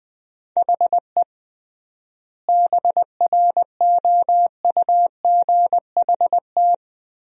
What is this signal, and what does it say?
Morse code, heard as a keyed tone, about 21 wpm, 700 Hz. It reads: HE BROUGHT
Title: 3x Word-Spacing - Sentences from Top 500 Words - 20wpm